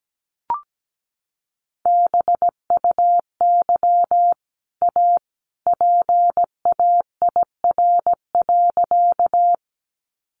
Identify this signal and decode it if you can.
Morse code, heard as a keyed tone, about 17 wpm, 700 Hz. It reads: BUY A PAIR.
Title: Sentences from Top 1200 Words 17wpm